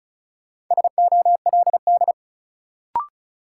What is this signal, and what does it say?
Morse code, heard as a keyed tone, about 36 wpm, 700 Hz. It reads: SOLD
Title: Sentences from Top 1400 Words - New Word Drill Alt - 35wpm